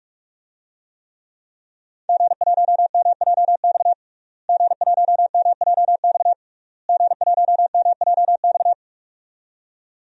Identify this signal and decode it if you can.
Morse code, heard as a keyed tone, about 45 wpm, 700 Hz. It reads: G1MJX G1MJX G1MJX
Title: DXCC-100 Call Signs - 5 Characters - Repeated 3x - 45wpm